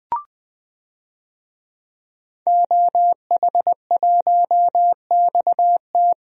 Morse code, heard as a keyed tone, about 20 wpm, 700 Hz. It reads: OH1XT
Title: DXCC-100 Call Signs - 5 Characters - Repeated 3x - 20wpm